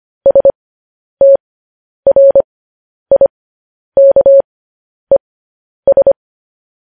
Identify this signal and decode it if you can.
Morse code, heard as a keyed tone, about 25 wpm, spaced as for 10 wpm, 550 Hz. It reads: STRIKES
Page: news headlines in morse code at 10 wpm